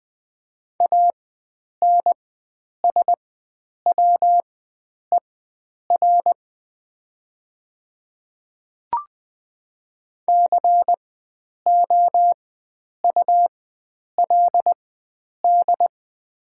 Morse code, heard as a keyed tone, about 20 wpm, spaced as for 9 wpm, 700 Hz. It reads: ANSWER COULD
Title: 4x Character Spacing - Top 500 Words - 20wpm